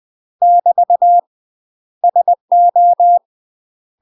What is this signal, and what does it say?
Morse code, heard as a keyed tone, about 20 wpm, 700 Hz. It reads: <BT> SO
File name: UK MASTER QSOs TRAINING / 20WPM / UK MASTER QSOs 0001